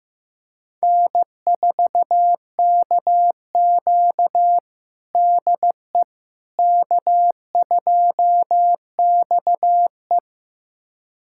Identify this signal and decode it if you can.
Morse code, heard as a keyed tone, about 15 wpm, 700 Hz. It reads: N4KQ DE K2XE
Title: USA Callsign DE Callsign - 4 Characters - 15wpm